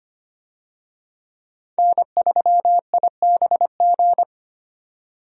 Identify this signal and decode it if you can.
Morse code, heard as a keyed tone, about 25 wpm, 700 Hz. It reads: N3IBG
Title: USA Call Signs - 5 Characters - 25wpm